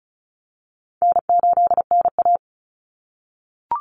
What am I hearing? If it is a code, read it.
Morse code, heard as a keyed tone, about 34 wpm, 700 Hz. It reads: N8NA
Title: Call Signs - 35wpm